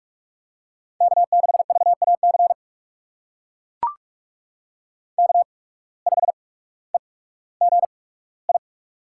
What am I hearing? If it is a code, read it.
Morse code, heard as a keyed tone, about 44 wpm, spaced as for 19 wpm, 700 Hz. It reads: K6VAC K5EGI
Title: USA Call Signs - 5 Characters - 8x Character Spacing - 45wpm